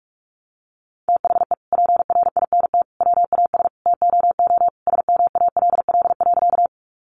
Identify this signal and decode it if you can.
Morse code, heard as a keyed tone, about 44 wpm, 700 Hz. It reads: THE PRINT WAS TOO SMALL.
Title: Sentences from Top 1400 Words 45wpm